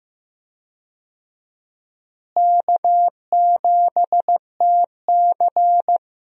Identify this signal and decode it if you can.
Morse code, heard as a keyed tone, about 15 wpm, 700 Hz. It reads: K7TC
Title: USA Call Signs - 4 Characters - Repeated 2x - 15wpm